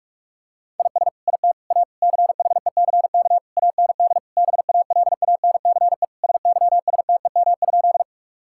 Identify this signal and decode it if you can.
Morse code, heard as a keyed tone, about 46 wpm, 700 Hz. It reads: IS IT A CHECK AND BALANCE SYSTEM?
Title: Sentences from Top 1000 Words - Review Part 6 - 45wpm